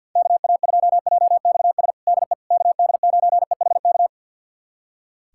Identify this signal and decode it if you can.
Morse code, heard as a keyed tone, about 51 wpm, 700 Hz. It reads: KA2JXS DE KD9EHK
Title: USA Callsign DE Callsign - 6 Characters - 50wpm